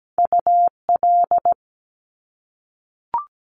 Morse code, heard as a keyed tone, about 17 wpm, 700 Hz. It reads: UL
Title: Top Two Letter Combinations in Words 17wpm